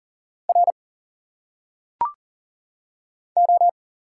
Morse code, heard as a keyed tone, about 40 wpm, 700 Hz. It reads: R O
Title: Single Letters - Phonetic Pronunciation 40wpm